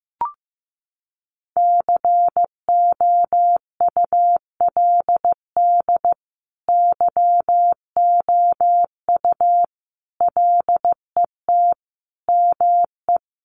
Morse code, heard as a keyed tone, about 15 wpm, 700 Hz. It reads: COULD YOU LET ME
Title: Sentences from Top 1400 Words 15wpm